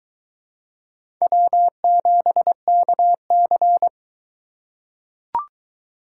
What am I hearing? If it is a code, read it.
Morse code, heard as a keyed tone, about 21 wpm, 700 Hz. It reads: W7KC